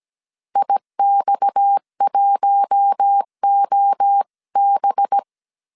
Morse code, heard as a keyed tone, about 17 wpm, 800 Hz. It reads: IX1OB
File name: 16wpm